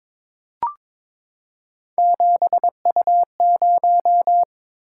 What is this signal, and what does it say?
Morse code, heard as a keyed tone, about 22 wpm, 700 Hz. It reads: 7U0
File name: practice / 22wpm / Three Letter-Number Combinations - 22wpm